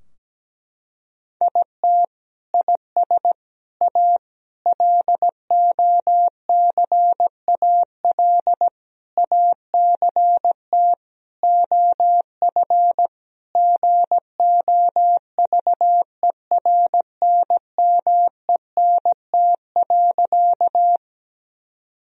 Morse code, heard as a keyed tone, about 17 wpm, 700 Hz. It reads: IT IS A LOCAL ACT OF GOVERNMENT.
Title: Sentences from Top 1000 Words - Review Part 3 - 17wpm